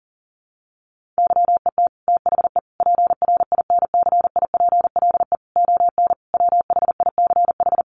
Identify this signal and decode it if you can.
Morse code, heard as a keyed tone, about 37 wpm, 700 Hz. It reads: YET THE PRINCIPLE ON WHICH
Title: Sentences from Top 1400 Words 40wpm